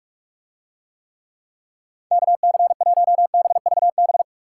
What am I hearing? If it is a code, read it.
Morse code, heard as a keyed tone, about 41 wpm, 700 Hz. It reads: KC1BVB